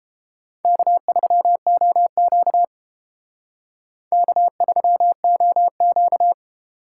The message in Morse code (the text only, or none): K3OQ K3OQ